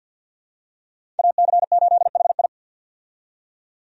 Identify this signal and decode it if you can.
Morse code, heard as a keyed tone, about 48 wpm, 700 Hz. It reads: AC8HI
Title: USA Call Signs - 5 Characters - 50wpm